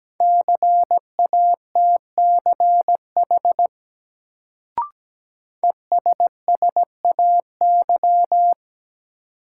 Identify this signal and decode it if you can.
Morse code, heard as a keyed tone, about 17 wpm, 700 Hz. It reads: CATCH ESSAY
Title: Sentences from Top 1500 Words - New Word Drill Alt - 17wpm